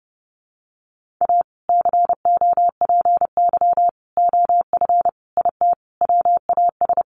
Morse code, heard as a keyed tone, about 29 wpm, 700 Hz. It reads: A COPY OF IT WAS
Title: Sentences from Top 1400 Words 30wpm